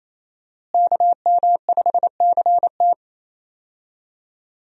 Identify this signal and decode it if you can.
Morse code, heard as a keyed tone, about 28 wpm, 700 Hz. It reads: KM5CT
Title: USA Call Signs - 5 Characters - Repeated 2x - 28wpm